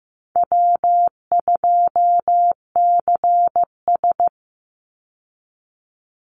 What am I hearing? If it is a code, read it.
Morse code, heard as a keyed tone, about 15 wpm, 700 Hz. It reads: W2CS
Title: Easy Call Signs Repeated 3x - 15wpm